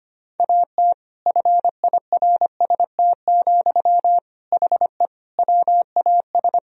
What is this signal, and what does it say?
Morse code, heard as a keyed tone, about 25 wpm, 700 Hz. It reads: AT FIRST, HE WAS
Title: Sentences from Top 1000 Words - Review Part 4 - 25wpm